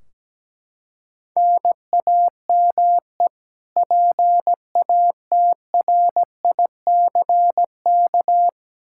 Morse code, heard as a keyed tone, about 17 wpm, 700 Hz. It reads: NAME PATRICK
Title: K1USN SST QSO Practice 17wpm